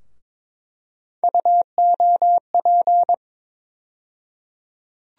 Morse code, heard as a keyed tone, about 22 wpm, 700 Hz. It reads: UOP
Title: USA Call Sign Postfixes - 3 Letters 22wpm